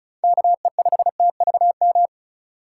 Morse code, heard as a keyed tone, about 36 wpm, 700 Hz. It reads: KE5TVM
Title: USA Call Signs - 6 Characters - 2x Character Spacing - 35wpm